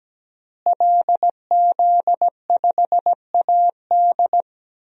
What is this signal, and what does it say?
Morse code, heard as a keyed tone, about 17 wpm, 700 Hz. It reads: LZ5AD
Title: DXCC-100 Call Signs - 5 Characters - Repeated 2x - 17wpm